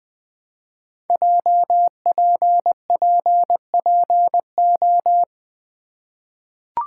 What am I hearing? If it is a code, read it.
Morse code, heard as a keyed tone, about 20 wpm, 700 Hz. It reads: JPPPO
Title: Code Groups - 5 Letters - 20wpm